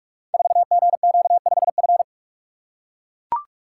Morse code, heard as a keyed tone, about 42 wpm, 700 Hz. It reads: 4GQ5F
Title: Code Groups - 5 Letter:Numbers - 45wpm